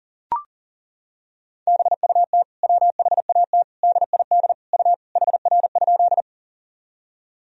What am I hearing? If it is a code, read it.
Morse code, heard as a keyed tone, about 40 wpm, 700 Hz. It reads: BUT WHAT DID U HR?
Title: Sentences from Top 300 Words and Abbreviations 40wpm